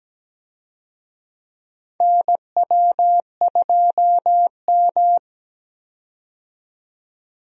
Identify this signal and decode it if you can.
Morse code, heard as a keyed tone, about 17 wpm, 700 Hz. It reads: NW2M
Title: USA Call Signs - 4 Characters - 17wpm